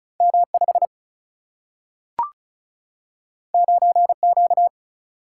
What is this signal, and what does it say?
Morse code, heard as a keyed tone, about 36 wpm, 700 Hz. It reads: M 5 9 Q
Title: Single-Letter Single-Number 35wpm